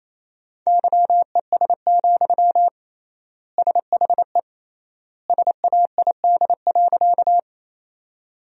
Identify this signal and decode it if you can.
Morse code, heard as a keyed tone, about 28 wpm, 700 Hz. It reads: YES, SHE SAID.
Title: 3x Word-Spacing - Sentences from Top 500 Words - 28wpm